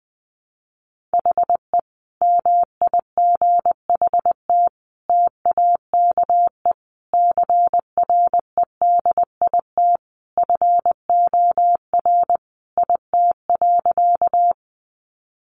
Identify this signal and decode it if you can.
Morse code, heard as a keyed tone, about 20 wpm, 700 Hz. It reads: HE MIGHT TAKE CREDIT FOR IT.
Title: Sentences from Top 1000 Words 20wpm